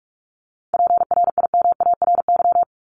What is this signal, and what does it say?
Morse code, heard as a keyed tone, about 44 wpm, 700 Hz. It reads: PRIMARY
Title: Top 501-750 Words 45wpm